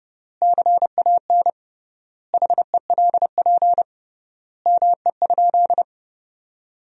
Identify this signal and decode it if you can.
Morse code, heard as a keyed tone, about 31 wpm, 700 Hz. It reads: CAN HELP ME?
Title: 3x Word-Spacing - Sentences from Top 200 Words - 30wpm